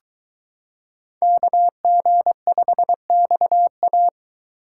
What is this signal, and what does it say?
Morse code, heard as a keyed tone, about 22 wpm, 700 Hz. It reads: KG5XA